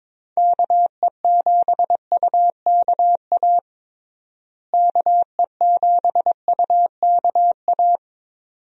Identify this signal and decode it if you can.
Morse code, heard as a keyed tone, about 22 wpm, 700 Hz. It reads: KE7UKA KE7UKA
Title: USA Call Signs - 6 Characters - Repeated 2x - 22wpm